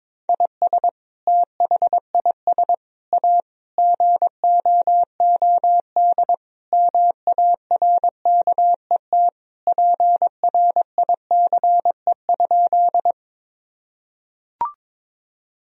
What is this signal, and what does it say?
Morse code, heard as a keyed tone, about 22 wpm, 700 Hz. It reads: IS THIS A GOOD MARKET PRICE?
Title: Sentences from Top 1000 Words - Review Part 3 - 22wpm